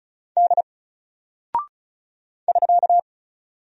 Morse code, heard as a keyed tone, about 36 wpm, 700 Hz. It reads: D <SK>
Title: Single Character (Letter:Number:Punctuation:Prosign) 35wpm